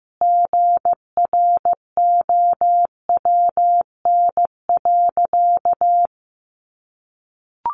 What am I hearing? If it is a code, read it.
Morse code, heard as a keyed tone, about 15 wpm, 700 Hz. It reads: GROWN.
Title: Easy Phrases 15wpm